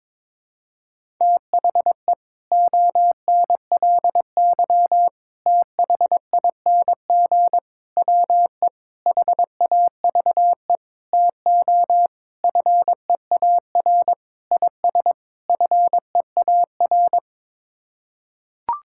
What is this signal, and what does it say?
Morse code, heard as a keyed tone, about 22 wpm, 700 Hz. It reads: THE ONLY THING WE HAVE TO FEAR IS FEAR
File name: Sets of 9 Words - 22wpm